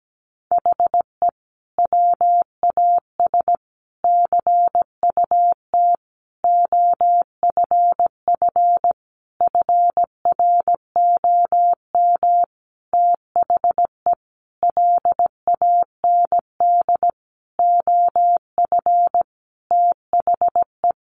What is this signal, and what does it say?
Morse code, heard as a keyed tone, about 17 wpm, 700 Hz. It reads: HE WAS CUT OFF FROM THE LAND OF THE
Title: Sentences from Top 1100 Words 17wpm